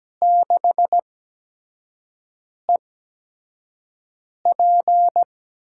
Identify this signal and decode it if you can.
Morse code, heard as a keyed tone, about 17 wpm, spaced as for 5 wpm, 700 Hz. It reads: 6EP
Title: USA Call Signs - 8x Character Spacing - 17wpm